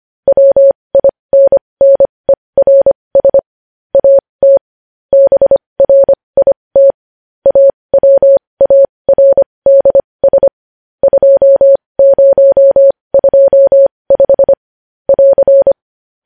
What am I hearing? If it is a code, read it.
Morse code, heard as a keyed tone, about 25 wpm, spaced as for 20 wpm, 550 Hz. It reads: WINNERS AT BRIT AWARDS 2025 <AR>
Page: news-headlines-in-morse-code-at-20-wpm